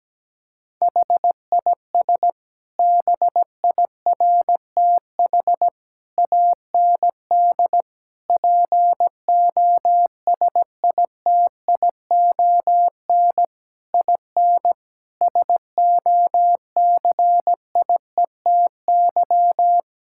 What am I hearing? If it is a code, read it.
Morse code, heard as a keyed tone, about 17 wpm, 700 Hz. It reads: HIS BIRTH AND POSITION IN SOCIETY
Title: Sentences from Top 1000 Words - Review Part 5 - 17wpm